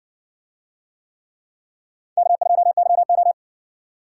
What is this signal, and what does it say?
Morse code, heard as a keyed tone, about 60 wpm, 700 Hz. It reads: K2YO